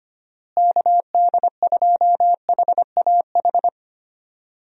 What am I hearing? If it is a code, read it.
Morse code, heard as a keyed tone, about 25 wpm, 700 Hz. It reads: KD2HAH
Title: USA Call Signs - 6 Characters - Repeated 3x - 25wpm